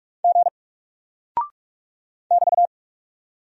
Morse code, heard as a keyed tone, about 44 wpm, 700 Hz. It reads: G <BT>
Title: Single Character (Letter:Number:Punctuation:Prosign) 45wpm